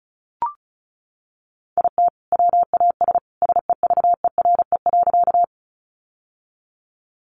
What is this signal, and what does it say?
Morse code, heard as a keyed tone, about 34 wpm, 700 Hz. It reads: IT WAS SEVERE.
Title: Sentences from Top 1400 Words 35wpm